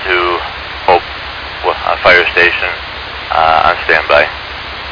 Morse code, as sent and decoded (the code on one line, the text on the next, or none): none